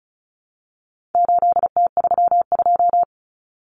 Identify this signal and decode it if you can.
Morse code, heard as a keyed tone, about 34 wpm, 700 Hz. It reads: 8T32
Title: Common Cut Numbers - 4-Digit 35wpm